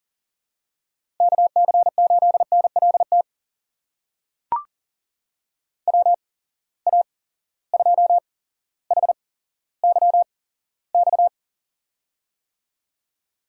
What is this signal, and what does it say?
Morse code, heard as a keyed tone, about 40 wpm, spaced as for 17 wpm, 700 Hz. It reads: KC8NLT WA2HYX
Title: USA Call Signs - 8x Character Spacing - 40wpm